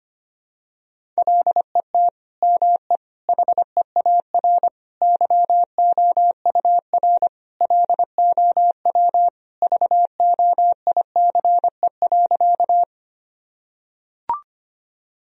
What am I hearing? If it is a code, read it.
Morse code, heard as a keyed tone, about 25 wpm, 700 Hz. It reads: LET ME HEAR YOUR LOW VOICE.
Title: Sentences from Top 1000 Words - Review Part 1 - 25wpm